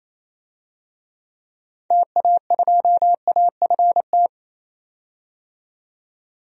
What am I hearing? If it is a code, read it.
Morse code, heard as a keyed tone, about 28 wpm, 700 Hz. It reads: TA2AFT
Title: DXCC-100 Call Signs - 6 Characters - 28wpm